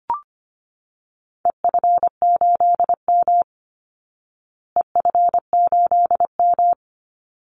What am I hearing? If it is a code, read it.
Morse code, heard as a keyed tone, about 25 wpm, 700 Hz. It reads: EF8M EF8M
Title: Call Signs Repeated 3x - 25wpm